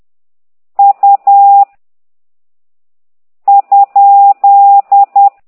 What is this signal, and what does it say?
Morse code, heard as a keyed tone, about 10 wpm, 800 Hz. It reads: U ?
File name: ct2idw / exerc13